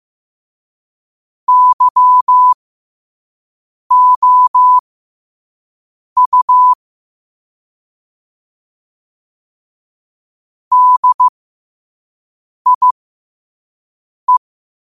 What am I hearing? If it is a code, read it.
Morse code, heard as a keyed tone, about 15 wpm, spaced as for 5 wpm, 1000 Hz. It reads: YOU DIE
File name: qotdfiles / 5wpm